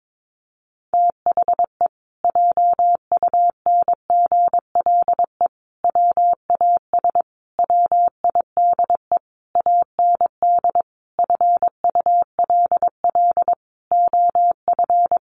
Morse code, heard as a keyed tone, about 22 wpm, 700 Hz. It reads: THE JUNGLE WAS WIDE AND FULL OF
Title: A Sound Of Thunder 22wpm